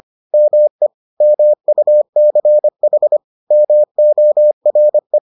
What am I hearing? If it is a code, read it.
Morse code, heard as a keyed tone, about 25 wpm, 600 Hz. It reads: ME MUCH MORE